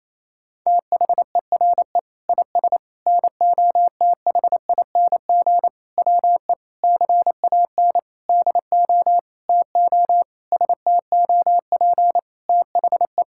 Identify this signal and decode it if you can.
Morse code, heard as a keyed tone, about 27 wpm, 700 Hz. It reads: THERE IS NOTHING WE CAN DO TO STOP THE
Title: Sentences from Top 1000 Words - Review Part 6 - 28wpm